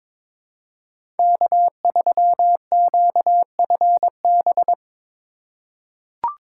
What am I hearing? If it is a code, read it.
Morse code, heard as a keyed tone, about 22 wpm, 700 Hz. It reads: K3QFB
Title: USA Call Signs - 5 Characters - 3x Character Spacing - 22wpm